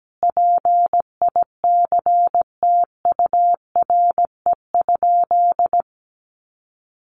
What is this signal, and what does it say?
Morse code, heard as a keyed tone, about 17 wpm, 700 Hz. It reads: PICTURE?
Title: A Sound Of Thunder 17wpm